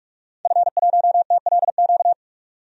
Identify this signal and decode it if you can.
Morse code, heard as a keyed tone, about 46 wpm, 700 Hz. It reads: F1TLQ
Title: DXCC-100 Call Signs - 5 Characters - 45wpm